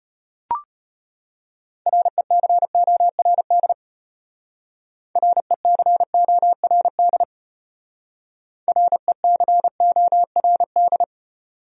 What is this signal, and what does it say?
Morse code, heard as a keyed tone, about 33 wpm, 700 Hz. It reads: RECORD RECORD RECORD